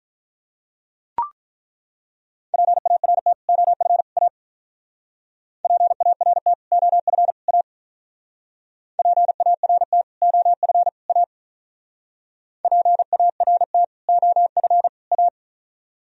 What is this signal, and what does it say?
Morse code, heard as a keyed tone, about 43 wpm, 700 Hz. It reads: PART OF A PART OF A PART OF A PART OF A
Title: Speed-Racing - Sets of 3 Words 35wpm